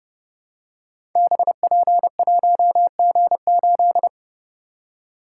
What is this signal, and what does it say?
Morse code, heard as a keyed tone, about 30 wpm, 700 Hz. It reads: BP1G8
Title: Code Groups - 5 Letter:Numbers - 30wpm